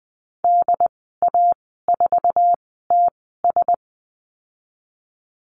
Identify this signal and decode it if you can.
Morse code, heard as a keyed tone, about 20 wpm, spaced as for 15 wpm, 700 Hz. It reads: DA4TS
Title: 069 Lesson 12g - Letter d - Call Signs 20wpm